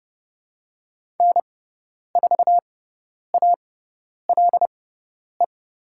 Morse code, heard as a keyed tone, about 31 wpm, spaced as for 10 wpm, 700 Hz. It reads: N4ALE
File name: Lesson 18g - Letter m - Call Signs 30wpm